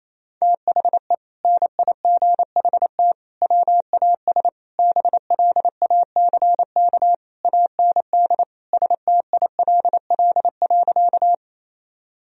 Morse code, heard as a keyed tone, about 28 wpm, 700 Hz. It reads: THE NIGHT WAS BLACK AND STILL.
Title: Sentences from Top 1000 Words - Review Part 2 - 28wpm